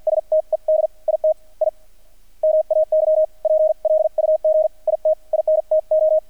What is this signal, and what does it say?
Morse code, heard as a keyed tone, about 48 wpm, spaced as for 42 wpm, 630 Hz, with radio noise. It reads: STED IT I MAY PLUG IT INTO
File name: archived / CRSnet